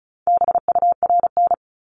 Morse code, heard as a keyed tone, about 36 wpm, 700 Hz. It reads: BURN
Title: Multi-letter Phonemes Repeated 3x 35wpm